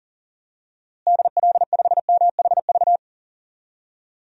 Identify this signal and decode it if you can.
Morse code, heard as a keyed tone, about 40 wpm, 700 Hz. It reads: DL5MHV
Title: DXCC-100 Call Signs - 6 Characters - Repeated 2x - 40wpm